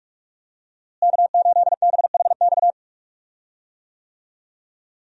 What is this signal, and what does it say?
Morse code, heard as a keyed tone, about 41 wpm, 700 Hz. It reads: K8BHX
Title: USA Call Signs - 5 Characters - Repeated 3x - 45wpm